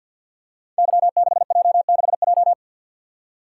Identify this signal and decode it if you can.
Morse code, heard as a keyed tone, about 50 wpm, 700 Hz. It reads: Y6J6J
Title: Code Groups - 5 Letter:Numbers - 50wpm